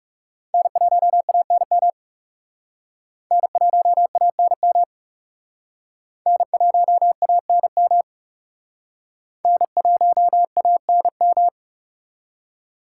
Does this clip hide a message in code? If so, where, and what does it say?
Yes: Morse code from 0.5 to 11.5 s, heard as a keyed tone, about 38 wpm, 700 Hz. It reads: N1ANM N1ANM N1ANM N1ANM